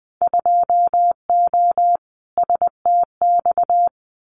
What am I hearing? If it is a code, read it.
Morse code, heard as a keyed tone, about 20 wpm, 700 Hz. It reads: 2O STX
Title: Winter Field Day Exchanges 20wpm